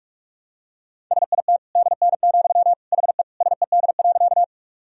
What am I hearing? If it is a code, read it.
Morse code, heard as a keyed tone, about 46 wpm, 700 Hz. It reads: SIT DN, HE SED.
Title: Sentences from Top 500 Words and Abbreviations 45wpm